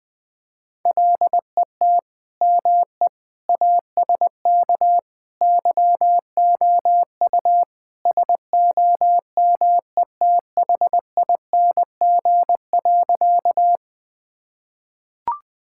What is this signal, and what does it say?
Morse code, heard as a keyed tone, about 20 wpm, 700 Hz. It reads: LET ME ASK YOU SOMETHING.
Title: Sentences with 4-Word Phrases - 20wpm